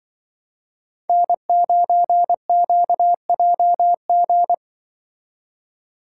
Morse code, heard as a keyed tone, about 24 wpm, 700 Hz. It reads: N9QJG
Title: Speed Racing - USA 1x3 Call Signs 20wpm